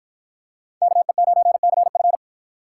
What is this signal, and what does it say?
Morse code, heard as a keyed tone, about 53 wpm, 700 Hz. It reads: KE9CF